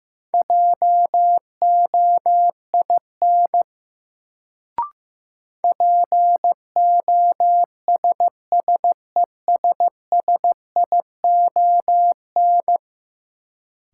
Morse code, heard as a keyed tone, about 15 wpm, 700 Hz. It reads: JOIN POSSESSION
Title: Sentences from Top 1400 Words - New Word Drill Alt - 15wpm